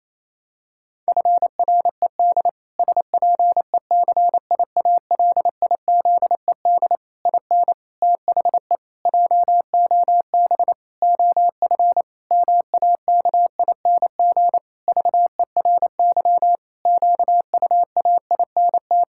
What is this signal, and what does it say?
Morse code, heard as a keyed tone, about 28 wpm, 700 Hz. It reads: FRED SPECIALIZED IN THE JOB OF MAKING VERY QUAINT
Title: Pangram Challenge v2 - 28wpm